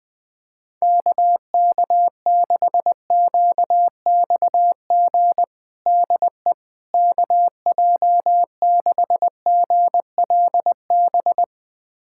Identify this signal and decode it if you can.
Morse code, heard as a keyed tone, about 21 wpm, 700 Hz. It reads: KK6QXG DE KJ6GLB
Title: USA Callsign DE Callsign - 6 Characters - 20wpm